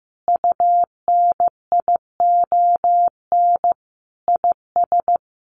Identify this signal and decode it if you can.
Morse code, heard as a keyed tone, about 15 wpm, 700 Hz. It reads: UNION IS
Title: Sentences from Top 1400 Words 15wpm